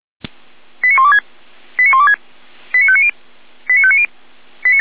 Five-tone selective call (ZVEI), heard as quick runs of five tones, tones of about 70 ms. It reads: 89126 89126 89590 89590